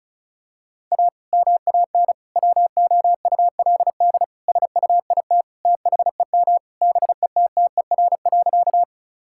Morse code, heard as a keyed tone, about 36 wpm, 700 Hz. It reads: A MAN WOULD SUIT THEM BETTER.
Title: Sentences from Top 1600 Words 35wpm